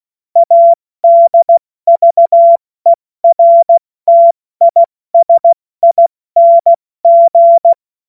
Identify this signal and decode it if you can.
Morse code, heard as a keyed tone, about 16 wpm, spaced as for 14 wpm, 680 Hz. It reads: ADVERTISING
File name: Morsekurs / Witze / Witz 27. September 2022 70 F80